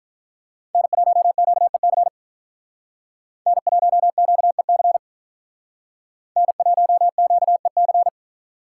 Morse code, heard as a keyed tone, about 44 wpm, 700 Hz. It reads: N1QEC N1QEC N1QEC